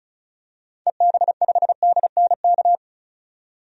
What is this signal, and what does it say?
Morse code, heard as a keyed tone, about 36 wpm, 700 Hz. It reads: EB5DNK
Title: DXCC-100 Call Signs - 6 Characters - 35wpm